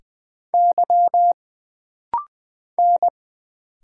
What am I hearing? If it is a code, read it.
Morse code, heard as a keyed tone, about 20 wpm, 700 Hz. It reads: Y N
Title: Single Letter-Number - Rapid-Fire 20wpm